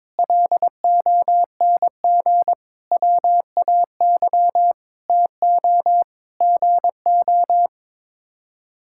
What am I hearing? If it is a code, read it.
Morse code, heard as a keyed tone, about 22 wpm, 700 Hz. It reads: LONG WAY TO GO
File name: Sets of 9 Words - 22wpm